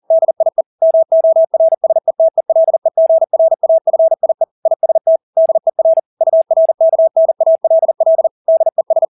Morse code, heard as a keyed tone, about 40 wpm, 650 Hz. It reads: DIE MORSETELEGRAFIE IST DER URKNALL DES
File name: dk5ke / Motto 200BpM